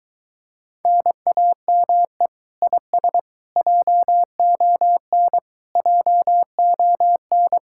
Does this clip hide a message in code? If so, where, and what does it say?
Yes: Morse code from 0.9 to 7.6 s, heard as a keyed tone, about 22 wpm, 700 Hz. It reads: NAME IS JON JON